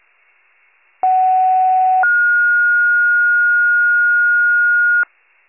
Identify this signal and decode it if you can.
Two-tone page, heard as a one-second tone, then a three-second tone, about 750 then 1430 Hz. It is heard from 1.0 to 5.0 s.
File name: 57 FIRE